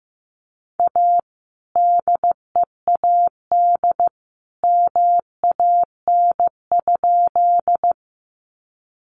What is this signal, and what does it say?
Morse code, heard as a keyed tone, about 15 wpm, 700 Hz. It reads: A DEAD MAN?
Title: Sentences from Top 1500 Words 15wpm